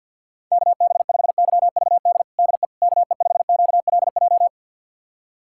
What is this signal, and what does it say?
Morse code, heard as a keyed tone, about 45 wpm, 700 Hz. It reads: KB5YVD DE KE5QLJ
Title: USA Callsign DE Callsign - 6 Characters - 50wpm